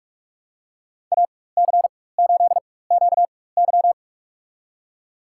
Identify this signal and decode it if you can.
Morse code, heard as a keyed tone, about 46 wpm, spaced as for 21 wpm, 700 Hz. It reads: AC8QY
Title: USA Call Signs - 5 Characters - 4x Character Spacing - 45wpm